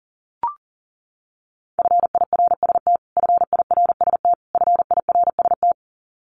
Morse code, heard as a keyed tone, about 40 wpm, 700 Hz. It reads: FIRST FIRST FIRST
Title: Top 200 Words Repeated 40wpm